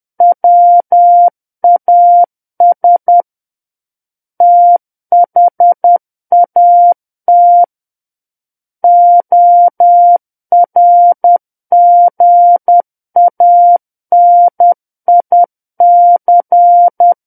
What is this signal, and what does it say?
Morse code, heard as a keyed tone, about 10 wpm, 700 Hz. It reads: WAS THAT ORGANIC
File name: Book One / 01-Chapter 1 - 10 WPM